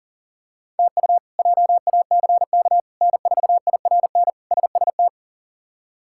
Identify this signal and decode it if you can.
Morse code, heard as a keyed tone, about 37 wpm, 700 Hz. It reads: TU JACK N4IRN SST
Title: K1USN SST QSO Practice - 40wpm